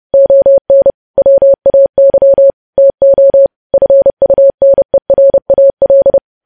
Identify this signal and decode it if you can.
Morse code, heard as a keyed tone, about 30 wpm, 550 Hz. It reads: ON WAY TO FUNERAL
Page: news headlines in morse code at 30 wpm